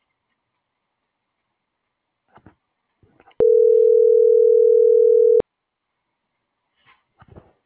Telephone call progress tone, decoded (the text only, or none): ringback tone